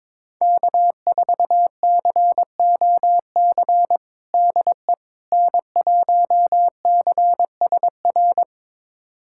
Morse code, heard as a keyed tone, about 21 wpm, 700 Hz. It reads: K4COC DE N1CSR
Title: USA Callsign DE Callsign 22wpm